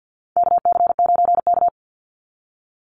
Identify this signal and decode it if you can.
Morse code, heard as a keyed tone, about 50 wpm, 700 Hz. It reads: KC9K
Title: Call Signs Repeated 2x - 50wpm